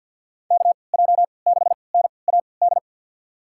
Morse code, heard as a keyed tone, about 50 wpm, spaced as for 28 wpm, 700 Hz. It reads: KJ6NAD
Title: USA Call Signs - 6 Characters - 3x Character Spacing - 50wpm